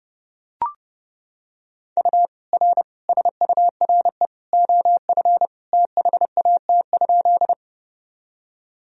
Morse code, heard as a keyed tone, about 32 wpm, 700 Hz. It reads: U R SURE OF THAT?
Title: Sentences from Top 300 Words and Abbreviations 30wpm